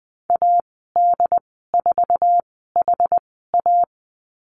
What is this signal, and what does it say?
Morse code, heard as a keyed tone, about 20 wpm, spaced as for 15 wpm, 700 Hz. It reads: AD4HA
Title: 088 Lesson 15g - Number 5 - Call Signs 20wpm